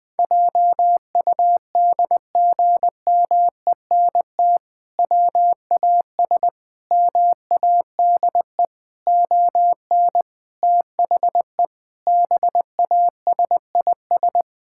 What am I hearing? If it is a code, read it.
Morse code, heard as a keyed tone, about 20 wpm, 700 Hz. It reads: JUDGMENT WAS MADE ON THE BASIS
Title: Sentences with 5-Word Phrases - Spoken Beginning - 20wpm